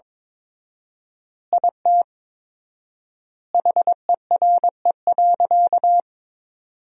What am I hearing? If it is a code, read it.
Morse code, heard as a keyed tone, about 22 wpm, 700 Hz. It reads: IT HERE.
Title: 4x Word-Spacing - Sentences from Top 400 Words - 22wpm